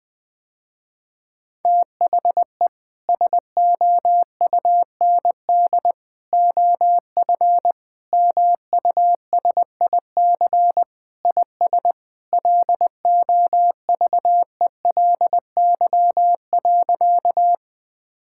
Morse code, heard as a keyed tone, about 20 wpm, 700 Hz. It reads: THE SOUND OF MUSIC IS LOVELY.
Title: Sentences with 3-Word Phrases - Spoken Beginning - 20wpm